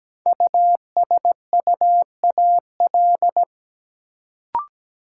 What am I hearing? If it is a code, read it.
Morse code, heard as a keyed tone, about 17 wpm, 700 Hz. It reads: USUAL
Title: Sentences from Top 400 Words - New Word Drill - 17wpm